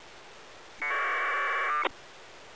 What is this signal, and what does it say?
Packet radio (AX.25, as in APRS).